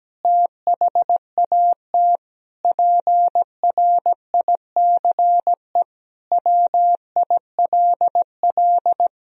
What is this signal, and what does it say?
Morse code, heard as a keyed tone, about 17 wpm, 700 Hz. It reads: THAT PRICE WILL
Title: Sentences from Top 1000 Words - Review Part 4 - 17wpm